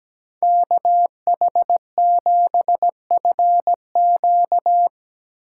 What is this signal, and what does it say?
Morse code, heard as a keyed tone, about 18 wpm, 700 Hz. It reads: KH7FQ
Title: Speed Racing - USA 2x2 Call Signs 17wpm